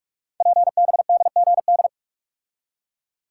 Morse code, heard as a keyed tone, about 44 wpm, 700 Hz. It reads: PBDGD
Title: Code Groups - 5 Letters - 45wpm